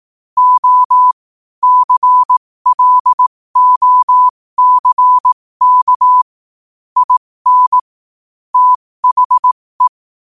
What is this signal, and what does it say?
Morse code, heard as a keyed tone, about 18 wpm, spaced as for 15 wpm, 1000 Hz. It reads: O CLOCK IN THE
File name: qotdfiles / 15wpm